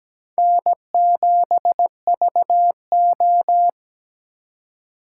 Morse code, heard as a keyed tone, about 20 wpm, 700 Hz. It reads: N7VO
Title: Speed Racing - USA 1x2 Call Signs 17wpm